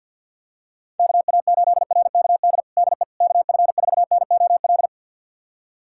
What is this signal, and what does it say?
Morse code, heard as a keyed tone, about 52 wpm, 700 Hz. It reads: KA8RKD DE KF4NOL
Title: USA Callsign DE Callsign - 6 Characters - 50wpm